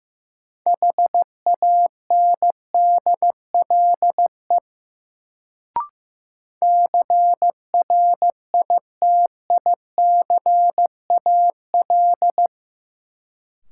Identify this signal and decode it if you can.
Morse code, heard as a keyed tone, about 15 wpm, 700 Hz. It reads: HANDLE CRITICAL
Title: Sentences from Top 1100 Words - New Word Drill Alt - 15wpm